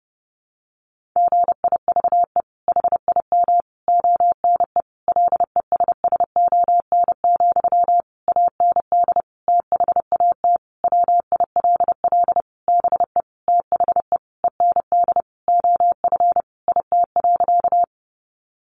Morse code, heard as a keyed tone, about 30 wpm, 700 Hz. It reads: GIVE HIM ONE LESSON, AND THAT WILL BE THE END OF IT.
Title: Sentences from Top 1200 Words 30wpm